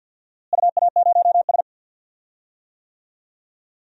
Morse code, heard as a keyed tone, about 45 wpm, 700 Hz. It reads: UA0S
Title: DXCC-100 Call Signs - 4 Characters - Repeated 3x - 50wpm